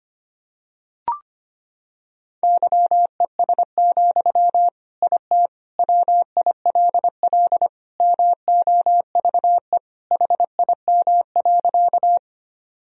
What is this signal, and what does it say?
Morse code, heard as a keyed tone, about 25 wpm, 700 Hz. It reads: YES, IT WILL MOVE HIM.
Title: Sentences from Top 1000 Words - Review Part 1 - 25wpm